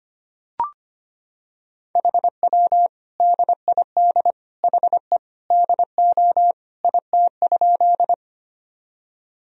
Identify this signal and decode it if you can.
Morse code, heard as a keyed tone, about 25 wpm, 700 Hz. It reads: HW DID HE DO IT?
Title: Sentences from Top 100 Words and Abbreviations 25wpm